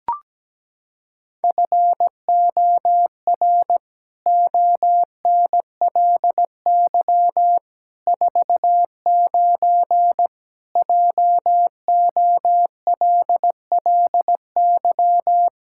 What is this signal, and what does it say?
Morse code, heard as a keyed tone, about 17 wpm, 700 Hz. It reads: FOR ONLY 49 JOLLY
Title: Pangram Challenge v2 - 17wpm